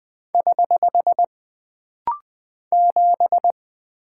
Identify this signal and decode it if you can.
Morse code, heard as a keyed tone, about 20 wpm, 700 Hz. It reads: <HH> 7
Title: Single Character (Letter:Number:Punctuation:Prosign) - Rapid-Fire - 20wpm